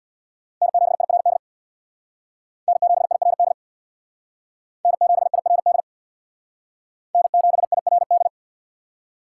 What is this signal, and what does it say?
Morse code, heard as a keyed tone, about 60 wpm, 700 Hz. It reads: N7IRD N7IRD N7IRD N7IRD
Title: Speed Racing - USA 1x3 Call Signs 50wpm